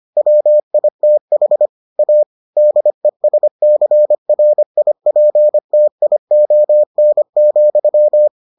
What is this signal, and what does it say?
Morse code, heard as a keyed tone, about 25 wpm, 600 Hz. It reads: WITH A DESCRIPTION,